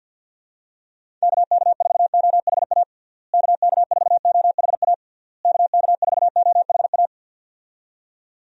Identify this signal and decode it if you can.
Morse code, heard as a keyed tone, about 47 wpm, 700 Hz. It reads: KK4OHA KK4OHA KK4OHA
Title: USA Call Signs - 6 Characters - Repeated 3x - 50wpm